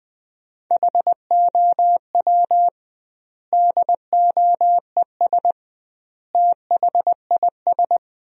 Morse code, heard as a keyed tone, about 20 wpm, 700 Hz. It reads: HOW DOES THIS
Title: 2x Word-Spacing - Sentences from Top 400 Words - 20wpm